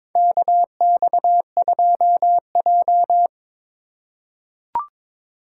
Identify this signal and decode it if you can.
Morse code, heard as a keyed tone, about 22 wpm, 700 Hz. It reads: KX2J
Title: USA Call Signs - 4 Characters - 3x Character Spacing - 22wpm